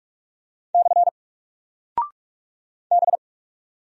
Morse code, heard as a keyed tone, about 44 wpm, 700 Hz. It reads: / B
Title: Single Character (Letter:Number:Punctuation:Prosign) 45wpm